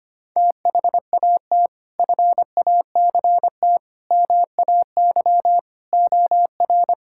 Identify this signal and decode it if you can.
Morse code, heard as a keyed tone, about 25 wpm, 700 Hz. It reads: THAT FACT MAY OR
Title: Sentences from Top 1000 Words - Review Part 6 - 25wpm